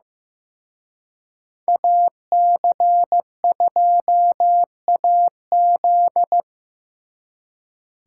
Morse code, heard as a keyed tone, about 18 wpm, 700 Hz. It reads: AC2AZ
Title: Speed Racing - USA 2x2 Call Signs 15wpm